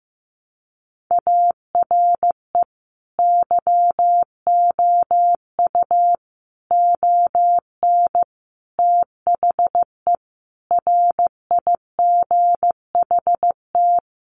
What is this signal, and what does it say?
Morse code, heard as a keyed tone, about 15 wpm, 700 Hz. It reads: ARE YOU ON THE RIGHT
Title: Sentences from Top 1100 Words 15wpm